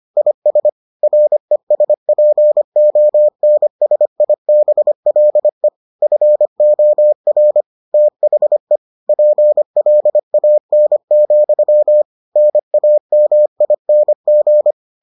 Morse code, heard as a keyed tone, about 25 wpm, 600 Hz. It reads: IS RESPONSIBLE FOR THE PLAN, NAMING